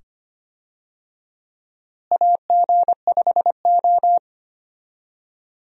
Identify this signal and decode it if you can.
Morse code, heard as a keyed tone, about 25 wpm, 700 Hz. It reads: AG5O
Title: USA Call Signs - 4 Characters - Repeated 2x - 25wpm